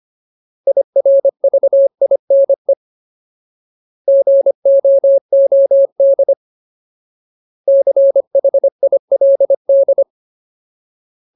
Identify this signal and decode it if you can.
Morse code, heard as a keyed tone, about 25 wpm, 550 Hz. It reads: IRVINE GOOD CHILD
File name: Ultimate Word List / 25WPM Ultimate